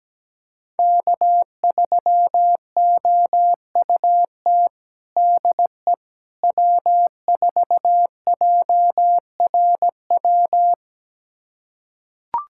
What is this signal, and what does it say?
Morse code, heard as a keyed tone, about 17 wpm, 700 Hz. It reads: K3OUT DE W4JRW
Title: USA Callsign DE Callsign - 5 Characters - 17wpm